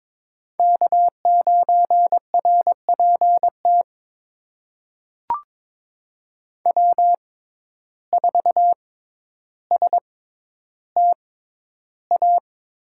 Morse code, heard as a keyed tone, about 22 wpm, spaced as for 11 wpm, 700 Hz. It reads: K9RPT W4STA
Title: USA Call Signs - 5 Characters - 6x Character Spacing - 22wpm